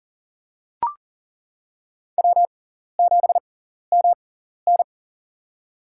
Morse code, heard as a keyed tone, about 39 wpm, spaced as for 14 wpm, 700 Hz. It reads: W7MN
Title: USA Call Signs - 6x Character Spacing - 40wpm